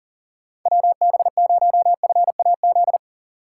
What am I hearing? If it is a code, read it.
Morse code, heard as a keyed tone, about 39 wpm, 700 Hz. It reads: WB0FAZ
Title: USA Call Signs - 6 Characters - 2x Character Spacing - 40wpm